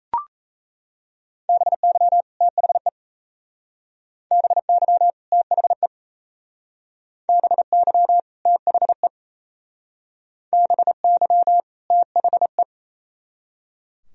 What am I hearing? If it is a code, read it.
Morse code, heard as a keyed tone, about 37 wpm, 700 Hz. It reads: BY THE BY THE BY THE BY THE